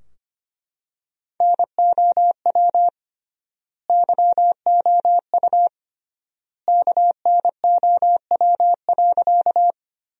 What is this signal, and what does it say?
Morse code, heard as a keyed tone, about 25 wpm, 700 Hz. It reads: NOW YOU KNOW.
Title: 3x Word-Spacing - Sentences from Top 100 Words - 25wpm